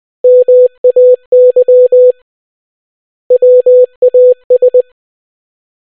Morse code, heard as a keyed tone, about 20 wpm, 500 Hz. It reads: MAY WAS